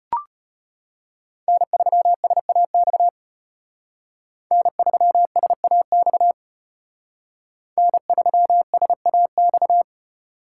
Morse code, heard as a keyed tone, about 33 wpm, 700 Hz. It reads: N3SAX N3SAX N3SAX